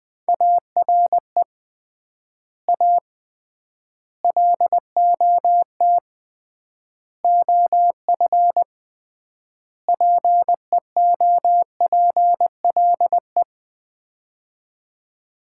Morse code, heard as a keyed tone, about 21 wpm, 700 Hz. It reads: ARE A LOT OF PEOPLE
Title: 3x Word-Spacing - Sets of 5 Words - 20wpm